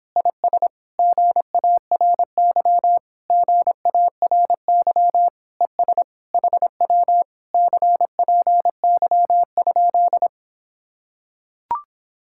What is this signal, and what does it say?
Morse code, heard as a keyed tone, about 26 wpm, 700 Hz. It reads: IS GARY GARY ES HW CPY?